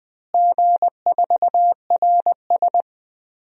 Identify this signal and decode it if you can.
Morse code, heard as a keyed tone, about 19 wpm, 700 Hz. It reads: G4RS
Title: DXCC-100 Call Signs - 4 Characters - Repeated 2x - 20wpm